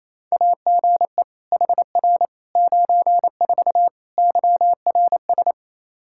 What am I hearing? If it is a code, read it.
Morse code, heard as a keyed tone, about 28 wpm, 700 Hz. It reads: AGE HR 94 YRS
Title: Practice QSO - Exchange 3 - More Info - 28wpm